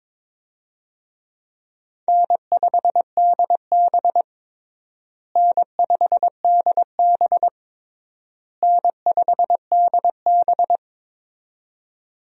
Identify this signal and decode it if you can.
Morse code, heard as a keyed tone, about 22 wpm, 700 Hz. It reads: N5DB N5DB N5DB
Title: USA Call Signs - 4 Characters - Repeated 3x - 22wpm